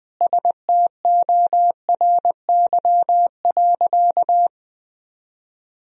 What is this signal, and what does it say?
Morse code, heard as a keyed tone, about 20 wpm, 700 Hz. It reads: STORY.
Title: Sentences from Top 300 Words and Abbreviations 20wpm